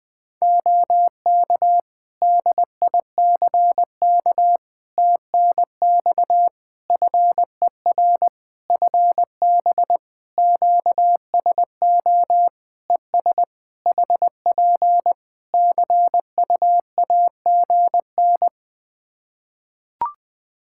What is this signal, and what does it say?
Morse code, heard as a keyed tone, about 20 wpm, 700 Hz. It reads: OK DICK TNX FER FB QSO ES HP CUAGN
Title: Practice QSO - Exchange 4 - Ending - 20wpm